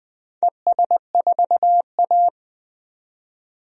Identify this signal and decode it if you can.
Morse code, heard as a keyed tone, about 20 wpm, 700 Hz. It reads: ES4A
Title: DXCC-100 Call Signs - 4 Characters - 20wpm